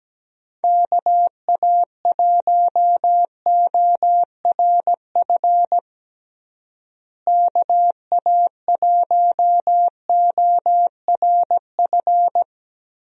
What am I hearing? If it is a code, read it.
Morse code, heard as a keyed tone, about 17 wpm, 700 Hz. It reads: KA1ORF KA1ORF
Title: USA Call Signs - 6 Characters - Repeated 3x - 17wpm